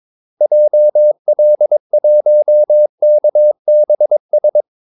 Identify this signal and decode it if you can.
Morse code, heard as a keyed tone, about 22 wpm, 600 Hz. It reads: JL1KBS
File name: Callsigns-22wpm